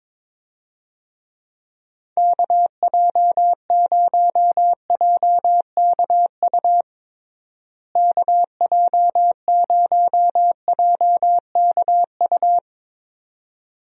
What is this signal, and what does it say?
Morse code, heard as a keyed tone, about 22 wpm, 700 Hz. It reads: KJ0JKU KJ0JKU
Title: USA Call Signs - 6 Characters - Repeated 2x - 22wpm